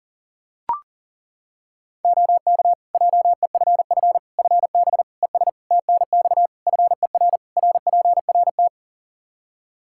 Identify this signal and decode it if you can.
Morse code, heard as a keyed tone, about 38 wpm, 700 Hz. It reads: OK JEFF FB ES TNX FER RPRT
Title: K1USN SST QSO Practice 40wpm